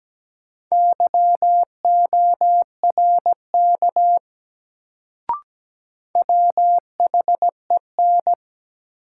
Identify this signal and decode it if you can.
Morse code, heard as a keyed tone, about 17 wpm, 700 Hz. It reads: YORK WHEN
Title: Sets of 2 Words - New Word Drill - 17wpm